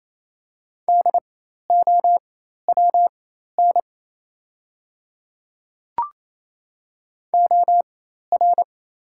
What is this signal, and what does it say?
Morse code, heard as a keyed tone, about 28 wpm, spaced as for 13 wpm, 700 Hz. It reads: DOWN OR
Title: 4x Character Spacing - Top 100 Words - 28wpm